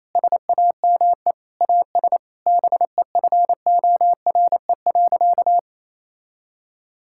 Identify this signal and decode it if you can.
Morse code, heard as a keyed tone, about 28 wpm, 700 Hz. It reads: SAME AS BEFORE.
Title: Sentences from Top 1000 Words - Review Part 5 - 28wpm